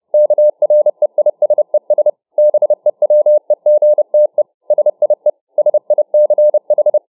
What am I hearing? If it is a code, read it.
Morse code, heard as a keyed tone, about 30 wpm, 600 Hz. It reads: KREISES BEWEGTE SIE SICH